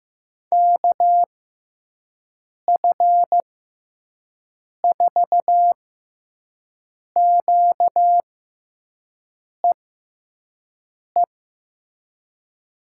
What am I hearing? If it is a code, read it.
Morse code, heard as a keyed tone, about 15 wpm, spaced as for 5 wpm, 700 Hz. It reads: KF4QEE
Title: USA Call Signs - 6x Character Spacing - 15wpm